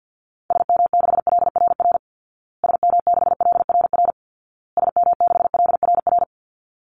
Morse code, heard as a keyed tone, about 50 wpm, 700 Hz. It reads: SM6LRR SM6LRR SM6LRR
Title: Call Signs Repeated 2x - 50wpm